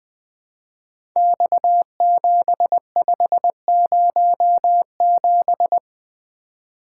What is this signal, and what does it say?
Morse code, heard as a keyed tone, about 21 wpm, 700 Hz. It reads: X7507
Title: Code Groups - 5 Letter:Numbers - 20wpm